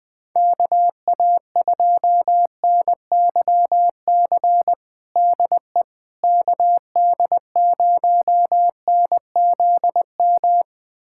Morse code, heard as a keyed tone, about 20 wpm, 700 Hz. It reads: KA2NYC DE KD0NZM
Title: USA Callsign DE Callsign - 6 Characters - 20wpm